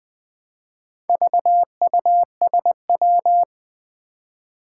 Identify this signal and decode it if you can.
Morse code, heard as a keyed tone, about 20 wpm, 700 Hz. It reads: VUSW
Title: Four Challenging Letter-Number Combinations 20wpm